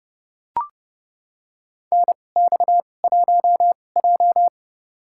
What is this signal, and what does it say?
Morse code, heard as a keyed tone, about 31 wpm, spaced as for 22 wpm, 700 Hz. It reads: NX1J
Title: USA Call Signs - 2x Character Spacing - 30wpm